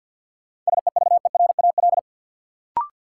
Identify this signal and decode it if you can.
Morse code, heard as a keyed tone, about 53 wpm, 700 Hz. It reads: SEVERAL
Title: Sentences from Top 400 Words - New Word Drill - 50wpm